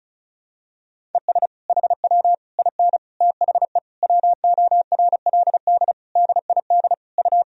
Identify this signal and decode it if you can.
Morse code, heard as a keyed tone, about 34 wpm, 700 Hz. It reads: ES HW IN THE WORLD DID U
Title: Sentences from Top 200 Words and Abbreviations 35wpm